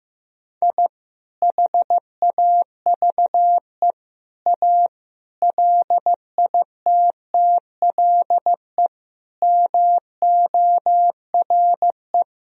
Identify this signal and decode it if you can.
Morse code, heard as a keyed tone, about 16 wpm, 700 Hz. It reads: I HAVE A LITTLE MORE
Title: Sentences from Top 1000 Words - Review Part 1 - 15wpm